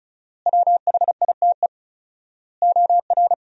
Morse code, heard as a keyed tone, about 36 wpm, 700 Hz. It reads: WHITE OR
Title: 4x Word-Spacing - Sentences from Top 400 Words - 35wpm